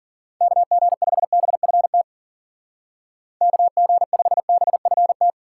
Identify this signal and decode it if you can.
Morse code, heard as a keyed tone, about 50 wpm, 700 Hz. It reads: KG5BFT KG5BFT
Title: Speed Racing - USA 2x3 Call Signs 40wpm